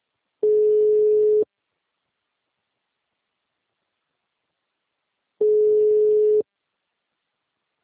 Ringback tone (Europe, 425 Hz).
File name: out- -2010- - - .142